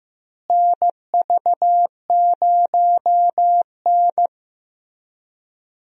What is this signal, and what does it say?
Morse code, heard as a keyed tone, about 15 wpm, 700 Hz. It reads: NV0N
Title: USA Call Signs - 4 Characters - Repeated 3x - 15wpm